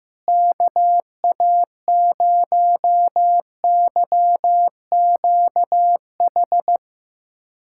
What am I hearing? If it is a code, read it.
Morse code, heard as a keyed tone, about 15 wpm, 700 Hz. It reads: KA0YQH
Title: USA Callsign DE Callsign 15wpm